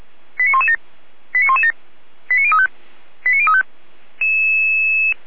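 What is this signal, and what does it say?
Five-tone selective call (ZVEI), heard as quick runs of five tones, tones of about 70 ms. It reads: 89117 89117 89035 89035
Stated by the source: ZVEI five-tone selective call 89115 received by SDR.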